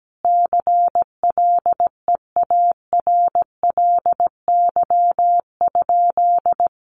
Morse code, heard as a keyed tone, about 17 wpm, 700 Hz. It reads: CLEARLY?
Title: Sentences from Top 1300 Words 17wpm